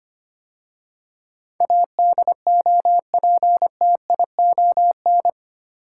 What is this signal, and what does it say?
Morse code, heard as a keyed tone, about 25 wpm, 700 Hz. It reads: ADOPTION
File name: 8 Letter Words - First 3 Letters Given - 25wpm